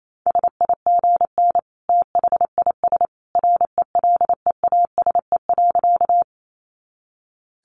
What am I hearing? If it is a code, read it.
Morse code, heard as a keyed tone, about 28 wpm, 700 Hz. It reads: SIGN THIS RELEASE.
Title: A Sound Of Thunder 28wpm